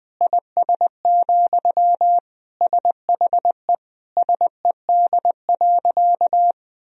Morse code, heard as a keyed tone, about 21 wpm, 700 Hz. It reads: IS, SHE SED.
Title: Sentences from Top 100 Words and Abbreviations 20wpm